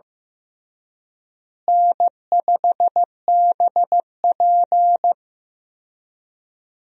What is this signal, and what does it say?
Morse code, heard as a keyed tone, about 16 wpm, 700 Hz. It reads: N5BP
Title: USA Call Signs - 4 Characters - Repeated 3x - 15wpm